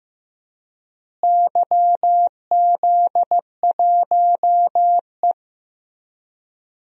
Morse code, heard as a keyed tone, about 15 wpm, 700 Hz. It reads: YZ1E
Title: DXCC-100 Call Signs - 4 Characters - Repeated 2x - 15wpm